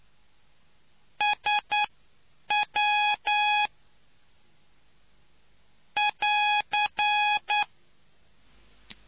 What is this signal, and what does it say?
Morse code, heard as a keyed tone, about 9 wpm, spaced as for 7 wpm, 840 Hz. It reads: SW <AR>